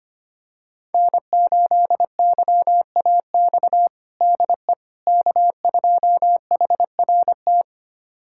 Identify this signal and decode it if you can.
Morse code, heard as a keyed tone, about 25 wpm, 700 Hz. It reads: N8YAX DE K2HRT
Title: USA Callsign DE Callsign - 5 Characters - 25wpm